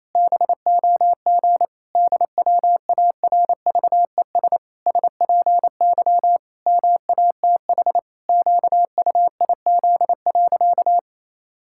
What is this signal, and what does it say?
Morse code, heard as a keyed tone, about 28 wpm, 700 Hz. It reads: BOG DWARVES SPY MATH QUIZ.
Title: Pangram Challenge v2 - 28wpm